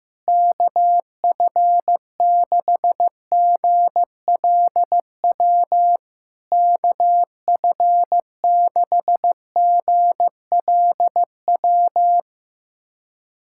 Morse code, heard as a keyed tone, about 15 wpm, 700 Hz. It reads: KF6GLW KF6GLW
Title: SOTA QSO Practice 15wpm